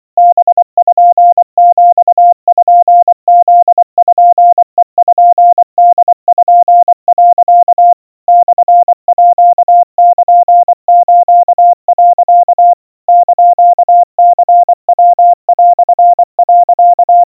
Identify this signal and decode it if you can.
Morse code (Japanese Wabun code), heard as a keyed tone, about 24 wpm, 700 Hz. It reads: パピプペポ、 モールス、 <KN>ニヤ)、
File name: iroha 80 cpm 700 Hz Crispy